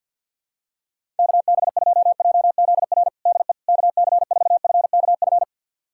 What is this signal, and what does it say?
Morse code, heard as a keyed tone, about 48 wpm, 700 Hz. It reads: KB2JZR DE KC4FKF
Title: USA Callsign DE Callsign - 6 Characters - 50wpm